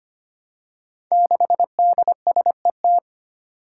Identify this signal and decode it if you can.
Morse code, heard as a keyed tone, about 25 wpm, 700 Hz. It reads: 6DSET
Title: Five Challenging Letter-Number Combinations 25wpm